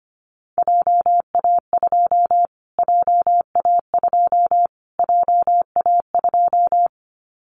Morse code, heard as a keyed tone, about 24 wpm, 700 Hz. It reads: JA2 JA2 JA2
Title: Callsign Prefixes for Top 25 DXCC - Repeated 3x - 25wpm